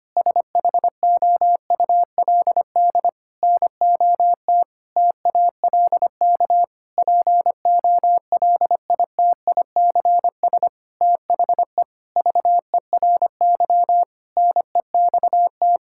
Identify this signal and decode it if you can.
Morse code, heard as a keyed tone, about 25 wpm, 700 Hz. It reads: SHOULD NOT TALK POLITICS THE VERY NEXT
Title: Sentences from Top 1000 Words - Review Part 6 - 25wpm